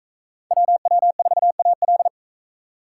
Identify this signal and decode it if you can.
Morse code, heard as a keyed tone, about 41 wpm, 700 Hz. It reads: WW4AL